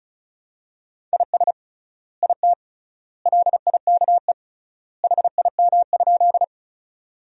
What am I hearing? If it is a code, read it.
Morse code, heard as a keyed tone, about 36 wpm, 700 Hz. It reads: IS IT LIKE HIM?
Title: 3x Word-Spacing - Sentences from Top 100 Words - 35wpm